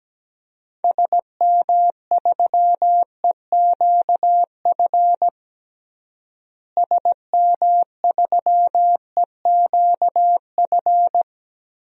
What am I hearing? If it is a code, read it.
Morse code, heard as a keyed tone, about 17 wpm, 700 Hz. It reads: SM3EQF SM3EQF
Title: DXCC-100 Call Signs - 6 Characters - Repeated 3x - 17wpm